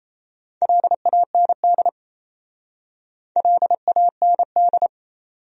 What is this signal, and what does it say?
Morse code, heard as a keyed tone, about 37 wpm, 700 Hz. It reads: LAND LAND